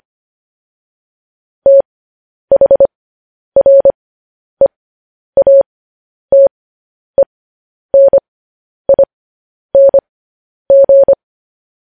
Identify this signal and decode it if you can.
Morse code, heard as a keyed tone, about 25 wpm, spaced as for 10 wpm, 550 Hz. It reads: THREATENING